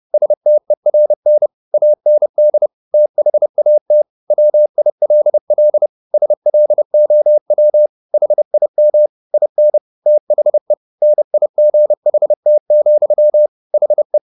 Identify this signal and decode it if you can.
Morse code, heard as a keyed tone, about 30 wpm, 600 Hz. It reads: STERN AND THAT WILL SLOW HIM IN THE NIGHT, HE